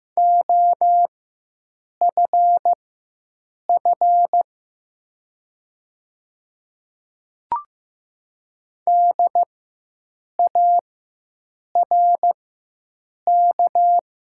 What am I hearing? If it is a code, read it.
Morse code, heard as a keyed tone, about 15 wpm, spaced as for 7 wpm, 700 Hz. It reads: OFF DARK
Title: 4x Character Spacing - Top 500 Words - 15wpm